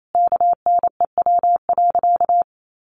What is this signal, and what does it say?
Morse code, heard as a keyed tone, about 28 wpm, 700 Hz. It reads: KNEW.
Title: Sentences from Top 1300 Words 28wpm